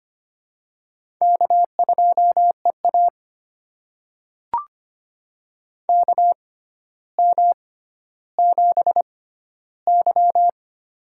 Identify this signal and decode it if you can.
Morse code, heard as a keyed tone, about 25 wpm, spaced as for 13 wpm, 700 Hz. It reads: K2EA KM7Y
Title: USA Call Signs - 4 Characters - 6x Character Spacing - 25wpm